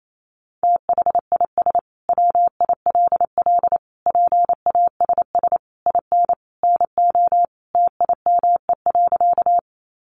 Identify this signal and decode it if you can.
Morse code, heard as a keyed tone, about 28 wpm, 700 Hz. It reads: THIS WILL PASS IN NO TIME.
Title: Sentences from Top 400 Words 28wpm